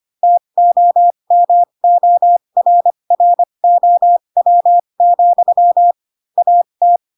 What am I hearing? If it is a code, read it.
Morse code, heard as a keyed tone, about 25 wpm, spaced as for 22 wpm, 700 Hz. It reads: TOMORROW, AT
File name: Pinocchio / Pinocchio ch4 18 25 ik1hge